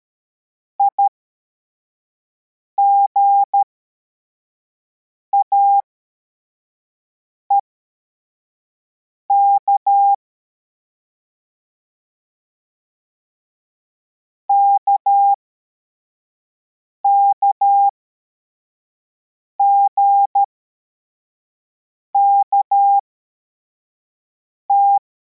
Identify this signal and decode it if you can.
Morse code, heard as a keyed tone, about 13 wpm, spaced as for 4 wpm, 800 Hz. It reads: IGAEK KKGKT
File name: formation / cw / Text34